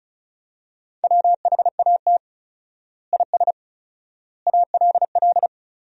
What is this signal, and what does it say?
Morse code, heard as a keyed tone, about 36 wpm, 700 Hz. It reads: WHAT IS ALL
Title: 4x Word-Spacing - Sentences from Top 100 Words - 35wpm